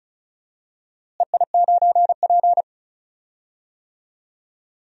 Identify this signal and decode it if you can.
Morse code, heard as a keyed tone, about 36 wpm, 700 Hz. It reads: EI9P
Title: DXCC-100 Call Signs - 4 Characters - Repeated 3x - 35wpm